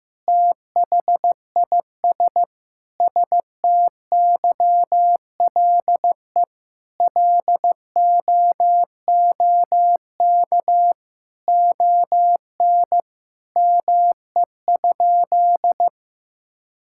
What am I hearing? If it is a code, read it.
Morse code, heard as a keyed tone, about 15 wpm, 700 Hz. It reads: THIS STYLE LOOK ON ME?
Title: Sentences from Top 1000 Words - Review Part 5 - 15wpm